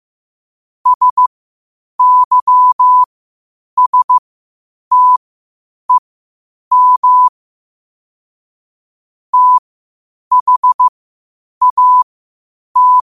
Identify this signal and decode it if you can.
Morse code, heard as a keyed tone, about 15 wpm, spaced as for 8 wpm, 1000 Hz. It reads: SYSTEM THAT
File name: qotdfiles / 8wpm